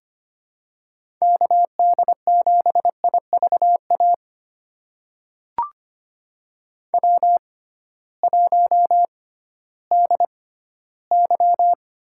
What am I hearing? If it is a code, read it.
Morse code, heard as a keyed tone, about 25 wpm, spaced as for 15 wpm, 700 Hz. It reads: KD7IVA W1DY
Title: USA Call Signs - 6x Character Spacing - 25wpm